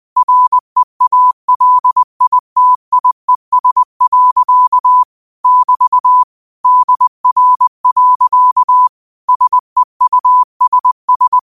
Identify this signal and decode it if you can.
Morse code, heard as a keyed tone, about 20 wpm, 1000 Hz. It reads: REALITIES. <BT> DR. SEUSS
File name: qotdfiles / 20wpm